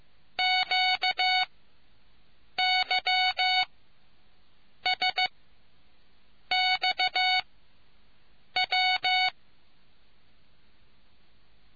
Morse code, heard as a keyed tone, about 15 wpm, spaced as for 6 wpm, 740 Hz. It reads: QYSXW